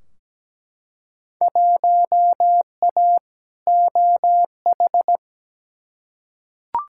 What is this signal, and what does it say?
Morse code, heard as a keyed tone, about 17 wpm, 700 Hz. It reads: 1A OH
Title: Field Day Exchanges - 17wpm